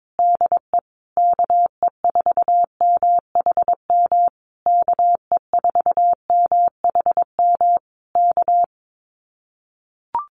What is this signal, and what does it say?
Morse code, heard as a keyed tone, about 22 wpm, 700 Hz. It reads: DE KE4MHM KE4MHM K
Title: Common QSO Phrases Encore 22wpm